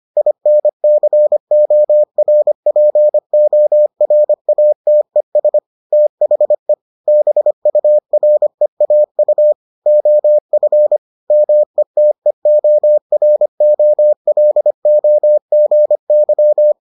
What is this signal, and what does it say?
Morse code, heard as a keyed tone, about 25 wpm, 600 Hz. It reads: INCORPORATES THE BUREAU OF METEOROLOGY